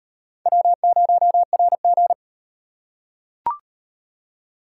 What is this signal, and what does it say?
Morse code, heard as a keyed tone, about 38 wpm, 700 Hz. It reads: W0RG